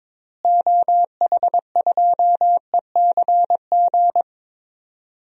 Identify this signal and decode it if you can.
Morse code, heard as a keyed tone, about 22 wpm, 700 Hz. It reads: OH2ECG
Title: DXCC-100 Call Signs - 6 Characters - Repeated 3x - 22wpm